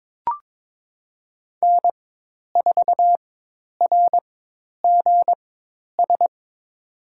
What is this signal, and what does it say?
Morse code, heard as a keyed tone, about 21 wpm, spaced as for 10 wpm, 700 Hz. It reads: N4RGS
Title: USA Call Signs - 5 Characters - 4x Character Spacing - 22wpm